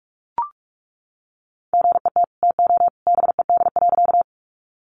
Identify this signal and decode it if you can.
Morse code, heard as a keyed tone, about 44 wpm, 700 Hz. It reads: GET TO BED.
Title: Sentences from Top 500 Words 45wpm